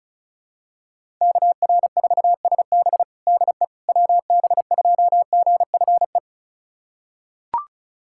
Morse code, heard as a keyed tone, about 36 wpm, 700 Hz. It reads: KR4SB DE WB2GFE
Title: USA Callsign DE Callsign 35wpm